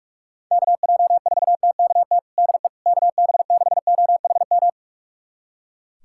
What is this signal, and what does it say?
Morse code, heard as a keyed tone, about 41 wpm, 700 Hz. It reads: KJ4TKT DE KB6OHM
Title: USA Callsign DE Callsign 45wpm